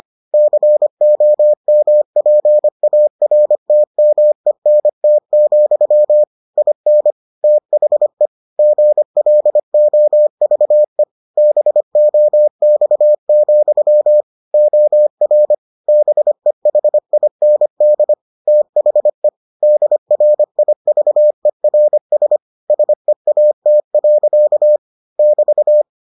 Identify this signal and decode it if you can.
Morse code, heard as a keyed tone, about 25 wpm, 600 Hz. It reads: COMPARTMENT, IN THE GLOVE BOX, OR BEHIND THE DRIVERS SEAT. <BT>